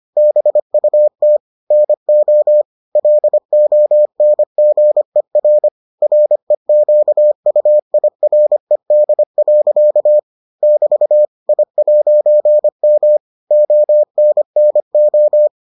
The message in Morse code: BUT NO LONGER REQUIRED. <BT> I'M ONNO

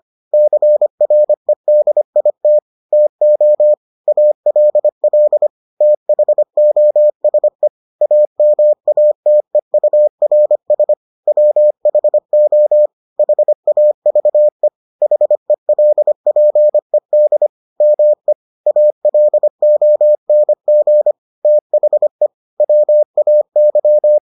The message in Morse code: CREDIT TO ALL THOSE AMATEURS WHO HAVE HELPED ME ALONG THE WAY

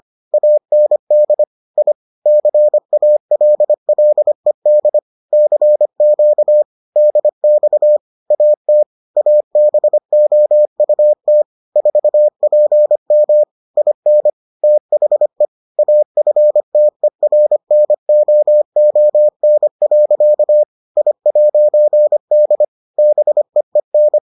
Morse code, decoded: AND I CALLED CQ DX AT ABOUT 4PM IN THE AFTERNOON. I'D BEEN

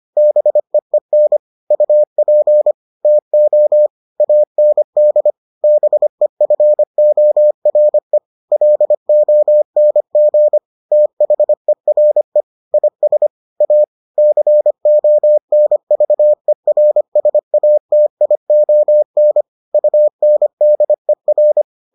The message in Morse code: BEEN UP TO AND BEFORE LONG THERE IS A CONVERSATION UNDER

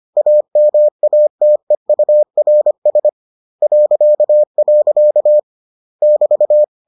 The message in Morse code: AMATEURS .. <BT>

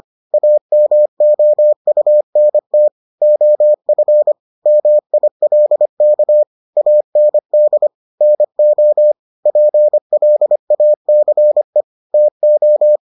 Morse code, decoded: AMOUNT OF MILK AND NO PLACE TO

